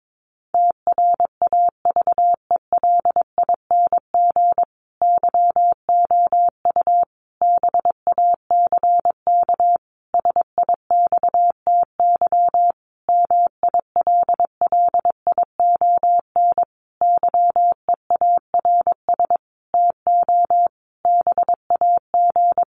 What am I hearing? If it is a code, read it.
Morse code, heard as a keyed tone, about 22 wpm, 700 Hz. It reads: TRAVELING YOU BACK SIXTY MILLION YEARS TO BAG
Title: A Sound Of Thunder 22wpm